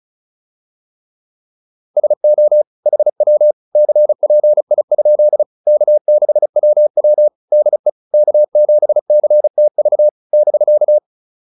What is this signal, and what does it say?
Morse code, heard as a keyed tone, about 35 wpm, 600 Hz. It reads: SO HW CPI? K6WW DE K7CTV <BK>
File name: QSO204 35